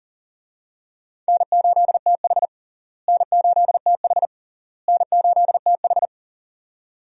Morse code, heard as a keyed tone, about 40 wpm, 700 Hz. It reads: N8TH N8TH N8TH
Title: USA Call Signs - 4 Characters - Repeated 3x - 40wpm